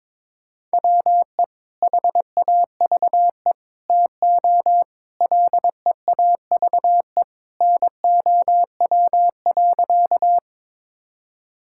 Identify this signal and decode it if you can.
Morse code, heard as a keyed tone, about 22 wpm, 700 Hz. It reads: WE HAVE TO LEAVE NOW.
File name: practice / 22wpm / Sentences with 3-Word Phrases - 22wpm